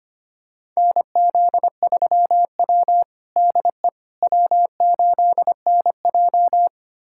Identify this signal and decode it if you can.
Morse code, heard as a keyed tone, about 25 wpm, 700 Hz. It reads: NZ3W DE W8NJ
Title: USA Callsign DE Callsign - 4 Characters - 25wpm